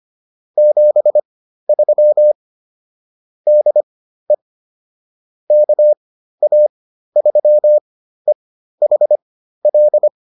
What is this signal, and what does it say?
Morse code, heard as a keyed tone, about 25 wpm, spaced as for 13 wpm, 600 Hz. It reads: 73 DE KA3EHL